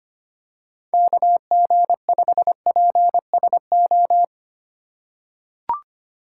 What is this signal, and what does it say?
Morse code, heard as a keyed tone, about 25 wpm, 700 Hz. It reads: KG5PSO
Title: USA Call Signs - 3x Character Spacing - 25wpm